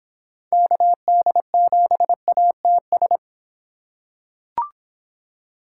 Morse code, heard as a keyed tone, about 25 wpm, 700 Hz. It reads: KD7ATS